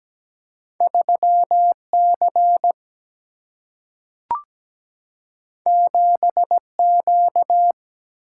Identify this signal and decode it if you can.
Morse code, heard as a keyed tone, about 17 wpm, 700 Hz. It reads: 3C 7Q
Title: Single-Letter Single-Number 17wpm